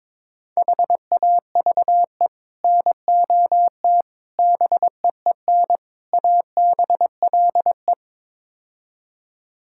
Morse code, heard as a keyed tone, about 22 wpm, 700 Hz. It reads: HAVE NOT BEEN ABLE
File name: practice / 22wpm / Build A Phrase - 5-Word Phrases - 22wpm